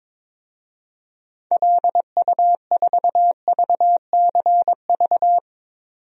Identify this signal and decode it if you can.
Morse code, heard as a keyed tone, about 22 wpm, 700 Hz. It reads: LU4VCV
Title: DXCC-100 Call Signs - 6 Characters - 22wpm